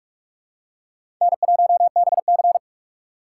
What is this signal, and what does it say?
Morse code, heard as a keyed tone, about 46 wpm, 700 Hz. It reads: N1BC
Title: USA Call Signs - 4 Characters - 3x Character Spacing - 45wpm